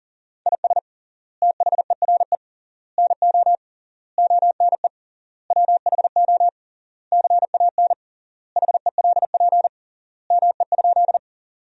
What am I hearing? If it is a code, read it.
Morse code, heard as a keyed tone, about 37 wpm, 700 Hz. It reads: IS THERE NO ONE WHO CAN HELP ME?
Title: 3x Word-Spacing - Sentences from Top 200 Words - 40wpm